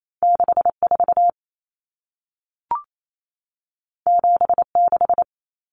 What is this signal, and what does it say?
Morse code, heard as a keyed tone, about 28 wpm, 700 Hz. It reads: 64 76
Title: Numbers - 2-Digit 28wpm